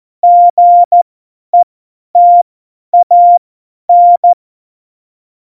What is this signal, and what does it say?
Morse code, heard as a keyed tone, about 14 wpm, spaced as for 10 wpm, 700 Hz. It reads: GETAN
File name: Morsekurs / Witze / Witz 06 March 2018, 50 F70